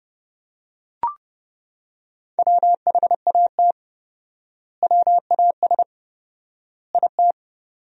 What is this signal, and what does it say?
Morse code, heard as a keyed tone, about 32 wpm, 700 Hz. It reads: WHAT WAS IT
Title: 4x Word-Spacing - Sentences from Top 300 Words - 30wpm